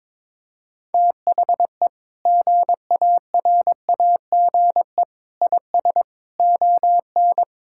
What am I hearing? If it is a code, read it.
Morse code, heard as a keyed tone, about 22 wpm, 700 Hz. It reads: THE GARAGE IS ON
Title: Sentences from Top 1600 Words 22wpm